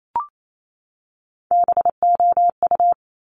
Morse code, heard as a keyed tone, about 28 wpm, 700 Hz. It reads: BOU
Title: Top Three Letter Combinations in Words 28wpm